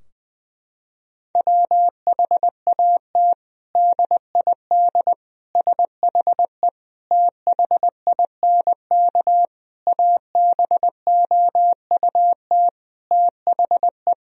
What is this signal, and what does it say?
Morse code, heard as a keyed tone, about 21 wpm, 700 Hz. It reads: WHAT DID SHE THINK ABOUT THE
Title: Sentences from Top 1000 Words - Review Part 4 - 20wpm